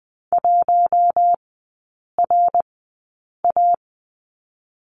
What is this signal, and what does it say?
Morse code, heard as a keyed tone, about 20 wpm, 700 Hz. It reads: 1 R A
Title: 114 Lesson 19e - Letter w - Three Characters 20wpm